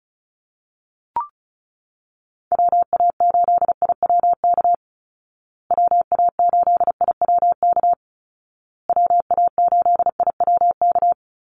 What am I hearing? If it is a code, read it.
Morse code, heard as a keyed tone, about 34 wpm, 700 Hz. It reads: WA8IWK WA8IWK WA8IWK
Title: Call Signs Repeated 2x - 35wpm